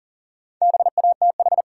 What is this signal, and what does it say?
Morse code, heard as a keyed tone, about 37 wpm, 700 Hz. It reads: BATH
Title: Sentences from Top 1600 Words - New Word Drill - 40wpm